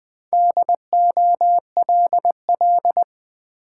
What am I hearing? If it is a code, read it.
Morse code, heard as a keyed tone, about 20 wpm, 690 Hz. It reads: DOLL
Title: Easy Words with Hints - 20wpm